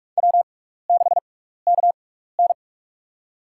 Morse code, heard as a keyed tone, about 46 wpm, spaced as for 15 wpm, 700 Hz. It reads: W6KN
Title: USA Call Signs - 4 Characters - 6x Character Spacing - 45wpm